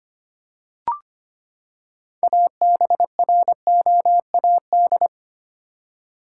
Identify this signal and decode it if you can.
Morse code, heard as a keyed tone, about 25 wpm, 700 Hz. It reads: ABROAD
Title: Six Letter Words - 25wpm